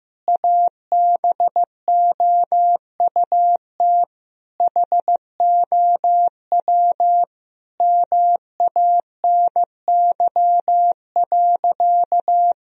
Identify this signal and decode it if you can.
Morse code, heard as a keyed tone, about 16 wpm, 700 Hz. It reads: ABOUT HOW MANY.
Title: Sentences from Top 1000 Words - Review Part 1 - 15wpm